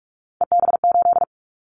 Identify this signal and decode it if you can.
Morse code, heard as a keyed tone, about 46 wpm, 690 Hz. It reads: EB8
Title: Callsign Prefixes for Top 50 DXCC - 45wpm